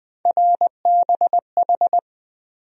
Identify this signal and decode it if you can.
Morse code, heard as a keyed tone, about 20 wpm, 700 Hz. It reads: RBH
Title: USA Call Sign Postfixes - 3 Letters 20wpm